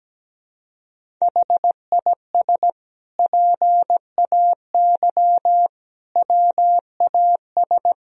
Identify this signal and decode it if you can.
Morse code, heard as a keyed tone, about 17 wpm, 700 Hz. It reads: HIS PAY WAS
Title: Sentences from Top 1000 Words - Review Part 1 - 17wpm